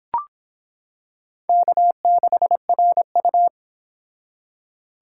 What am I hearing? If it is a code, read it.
Morse code, heard as a keyed tone, about 25 wpm, 700 Hz. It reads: K6RU